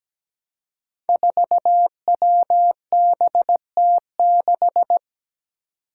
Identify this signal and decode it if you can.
Morse code, heard as a keyed tone, about 17 wpm, 700 Hz. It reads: 4WBT6
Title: Five Challenging Letter-Number Combinations 17wpm